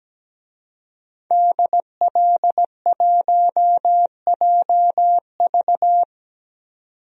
Morse code, heard as a keyed tone, about 17 wpm, 700 Hz. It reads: DL1JV
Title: DXCC-100 Call Signs - 5 Characters - 17wpm